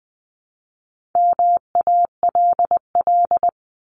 Morse code, heard as a keyed tone, about 20 wpm, 700 Hz. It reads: MALL
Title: Top 1501-1750 Words 20wpm